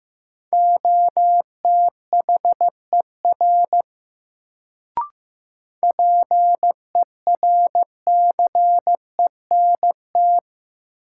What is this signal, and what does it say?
Morse code, heard as a keyed tone, about 15 wpm, 700 Hz. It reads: OTHER PERCENT
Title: Sets of 2 Words - New Word Drill - 15wpm